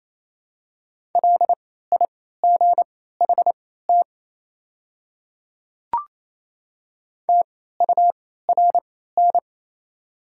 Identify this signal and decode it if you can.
Morse code, heard as a keyed tone, about 28 wpm, spaced as for 16 wpm, 700 Hz. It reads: LIGHT TURN
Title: 3x Character Spacing - Top 200 Words - 28wpm